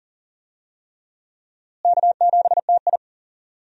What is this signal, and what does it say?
Morse code, heard as a keyed tone, about 40 wpm, 700 Hz. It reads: K7TI
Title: USA Call Signs - 4 Characters - Repeated 3x - 40wpm